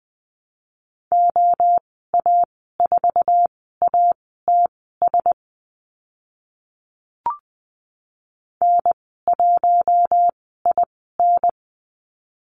Morse code, heard as a keyed tone, about 20 wpm, spaced as for 14 wpm, 700 Hz. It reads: OA4ATS N1IN
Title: 069 Lesson 12g - Letter d - Call Signs 20wpm